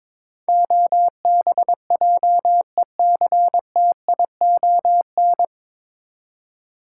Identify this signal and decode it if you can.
Morse code, heard as a keyed tone, about 22 wpm, 700 Hz. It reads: OBJECTION
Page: Morse Code Ninja practice 22wpm